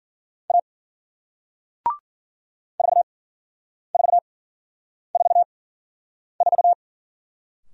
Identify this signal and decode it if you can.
Morse code, heard as a keyed tone, about 53 wpm, spaced as for 9 wpm, 700 Hz. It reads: A 4444